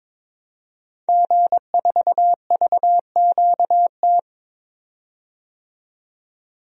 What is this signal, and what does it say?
Morse code, heard as a keyed tone, about 22 wpm, 700 Hz. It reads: G4VQT
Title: DXCC-100 Call Signs - 5 Characters - 22wpm